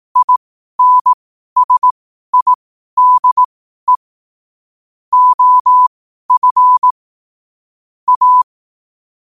Morse code, heard as a keyed tone, about 18 wpm, spaced as for 12 wpm, 1000 Hz. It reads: INSIDE OF A
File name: qotdfiles / 12wpm